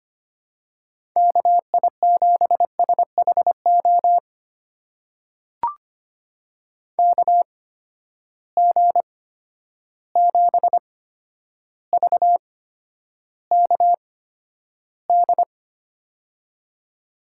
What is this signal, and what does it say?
Morse code, heard as a keyed tone, about 25 wpm, spaced as for 11 wpm, 700 Hz. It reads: KI7SHO KG7VKD
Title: USA Call Signs - 6 Characters - 8x Character Spacing - 25wpm